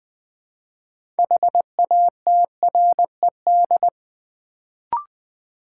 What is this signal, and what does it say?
Morse code, heard as a keyed tone, about 20 wpm, 700 Hz. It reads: HATRED
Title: Build A Word - 2 Letters At A Time - 6-Letter Words - 20wpm